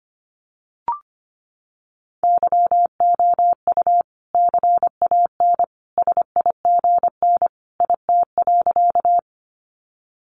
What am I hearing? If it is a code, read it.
Morse code, heard as a keyed tone, about 25 wpm, 700 Hz. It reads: YOU CAN SIGN IT.
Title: Sentences from Top 1000 Words 25wpm